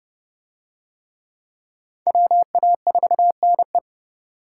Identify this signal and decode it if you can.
Morse code, heard as a keyed tone, about 30 wpm, 700 Hz. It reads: WA4NE